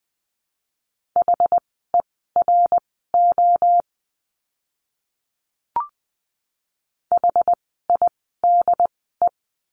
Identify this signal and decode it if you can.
Morse code, heard as a keyed tone, about 20 wpm, spaced as for 15 wpm, 700 Hz. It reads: HERO HIDE
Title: 068 Lesson 12f - Letter d - Words 20wpm